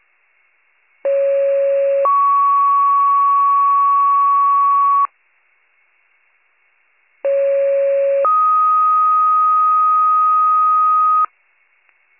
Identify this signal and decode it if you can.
Two-tone page, heard as a one-second tone, then a three-second tone, about 550 then 1120 Hz, about 550 then 1250 Hz. It is heard from 1.0 to 11.3 s.